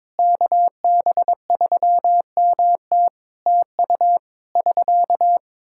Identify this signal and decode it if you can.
Morse code, heard as a keyed tone, about 21 wpm, 700 Hz. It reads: KB3MT TU <SK>
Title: Practice QSO - Exchange 4 - Ending - 22wpm